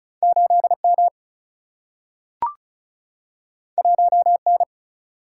Morse code, heard as a keyed tone, about 36 wpm, 700 Hz. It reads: 8M 1N
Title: Single-Letter Single-Number 35wpm